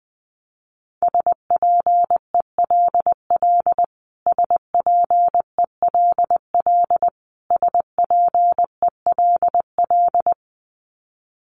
Morse code, heard as a keyed tone, about 19 wpm, 700 Hz. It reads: SPELL SPELL SPELL
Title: Top 500 Words Repeated 20wpm